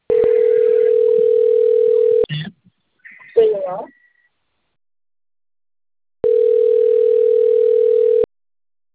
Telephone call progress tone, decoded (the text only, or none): ringback tone